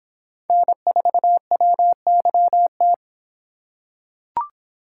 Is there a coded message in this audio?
Yes: Morse code from 0.5 to 2.9 s, heard as a keyed tone, about 26 wpm, 700 Hz. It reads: N4WYT